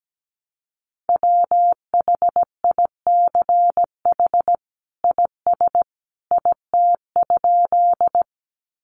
Morse code, heard as a keyed tone, about 17 wpm, 700 Hz. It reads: WHICH IS IT?
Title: Sentences from Top 100 Words 17wpm